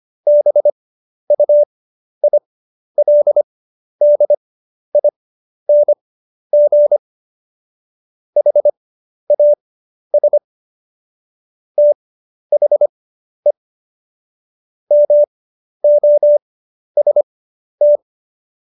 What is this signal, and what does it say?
Morse code, heard as a keyed tone, about 25 wpm, spaced as for 11 wpm, 590 Hz. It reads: BUILDING HAS THE MOST
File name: Jokes-25-char-10-eff